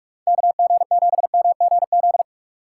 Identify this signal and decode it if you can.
Morse code, heard as a keyed tone, about 45 wpm, 700 Hz. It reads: KG7MGZ
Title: USA Call Signs - 6 Characters - 4x Character Spacing - 45wpm